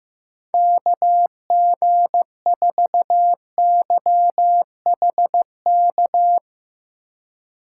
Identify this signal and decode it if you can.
Morse code, heard as a keyed tone, about 15 wpm, 700 Hz. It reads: KG4YHK
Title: USA Call Signs - 2x Character Spacing - 15wpm